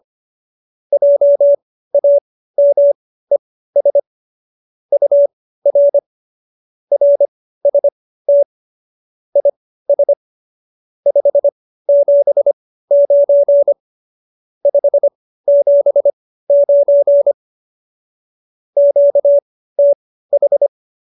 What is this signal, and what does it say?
Morse code, heard as a keyed tone, about 25 wpm, spaced as for 15 wpm, 570 Hz. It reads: JAMES UR RST IS 579 579 QTH